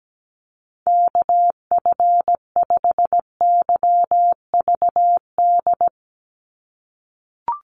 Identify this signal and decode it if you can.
Morse code, heard as a keyed tone, about 17 wpm, 700 Hz. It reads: KF5YVD
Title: USA Call Signs 17wpm